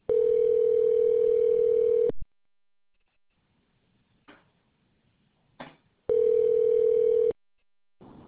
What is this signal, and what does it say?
Ringback tone (North America), 440 + 480 Hz.